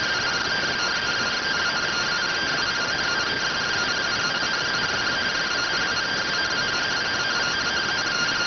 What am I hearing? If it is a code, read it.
Two-tone teleprinter signal, about 170 Hz shift.